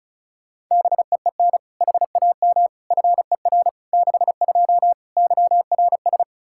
Morse code, heard as a keyed tone, about 36 wpm, 700 Hz. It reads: BEEN HAM FER 62 YRS
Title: K1USN SST QSO Practice 35wpm